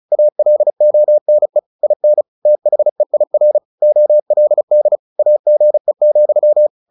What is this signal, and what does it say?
Morse code, heard as a keyed tone, about 35 wpm, 600 Hz. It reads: ALONE IN THEIR OLD AGE,
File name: the old man and the sea / the old man and the sea 1 35wpm